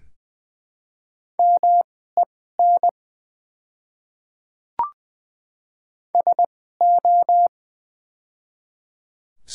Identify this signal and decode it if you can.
Morse code, heard as a keyed tone, about 20 wpm, spaced as for 14 wpm, 700 Hz. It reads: MEN SO
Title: 2x Character Spacing - Top 200 Words - 20wpm